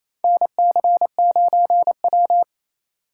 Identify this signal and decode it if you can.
Morse code, heard as a keyed tone, about 28 wpm, 700 Hz. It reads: NC9W
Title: K1USN SST QSO Practice - 28wpm